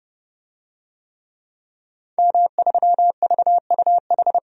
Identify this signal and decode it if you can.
Morse code, heard as a keyed tone, about 30 wpm, 700 Hz. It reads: M3VUH
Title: DXCC-100 Call Signs - 5 Characters - 30wpm